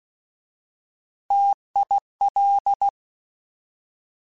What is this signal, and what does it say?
Morse code, heard as a keyed tone, about 16 wpm, 800 Hz. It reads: TIL